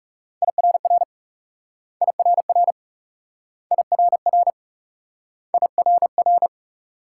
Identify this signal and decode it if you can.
Morse code, heard as a keyed tone, about 37 wpm, 700 Hz. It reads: IRR IRR IRR IRR
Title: Speed-Racing Multi-letter Phonemes - 30wpm